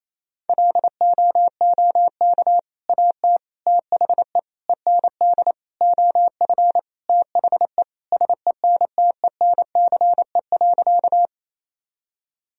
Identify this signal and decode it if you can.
Morse code, heard as a keyed tone, about 27 wpm, 700 Hz. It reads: LOOK AT THE END OF THE SENTENCE.
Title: Sentences from Top 1000 Words - Review Part 1 - 28wpm